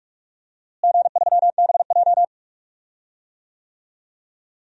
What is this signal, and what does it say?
Morse code, heard as a keyed tone, about 45 wpm, 700 Hz. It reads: G3BJ
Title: DXCC-100 Call Signs - 4 Characters - Repeated 2x - 45wpm